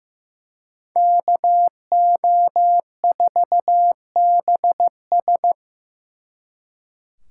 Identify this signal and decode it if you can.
Morse code, heard as a keyed tone, about 18 wpm, 700 Hz. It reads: KO4BS
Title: Speed Racing - USA 2x2 Call Signs 15wpm